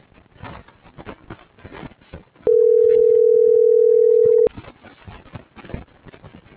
Ringback tone (North America), 440 + 480 Hz.